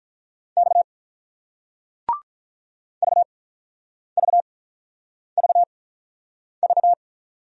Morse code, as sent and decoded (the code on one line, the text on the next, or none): -..- / ...- ...- ...- ...-
X VVVV